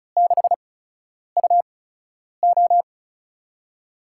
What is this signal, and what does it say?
Morse code, heard as a keyed tone, about 36 wpm, spaced as for 10 wpm, 700 Hz. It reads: 6UO
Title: USA Call Signs - 5 Characters - 8x Character Spacing - 35wpm